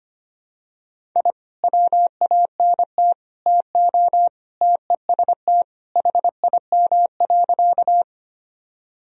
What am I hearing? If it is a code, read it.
Morse code, heard as a keyed tone, about 25 wpm, 700 Hz. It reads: I WANT TO TEST HIM.
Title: Sentences from Top 1000 Words - Review Part 3 - 25wpm